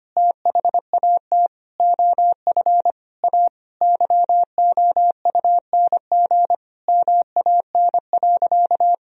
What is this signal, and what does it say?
Morse code, heard as a keyed tone, about 25 wpm, 700 Hz. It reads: THAT OF A YOUNG MAN.
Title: Sentences from Top 500 Words and Abbreviations 25wpm